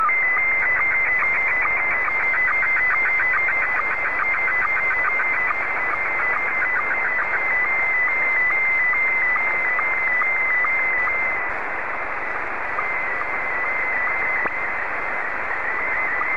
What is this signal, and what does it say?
SSTV (slow-scan television), Scottie 1.